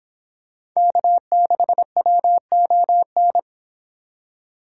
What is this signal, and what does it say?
Morse code, heard as a keyed tone, about 25 wpm, 700 Hz. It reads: K6WON